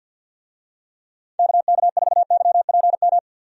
Morse code, heard as a keyed tone, about 51 wpm, 700 Hz. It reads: KK4YPM
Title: USA Call Signs - 6 Characters - 50wpm